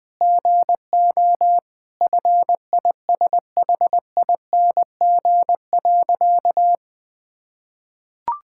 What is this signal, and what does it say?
Morse code, heard as a keyed tone, about 21 wpm, 700 Hz. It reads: GO FISHING.
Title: Sentences from Top 1000 Words - Review Part 6 - 20wpm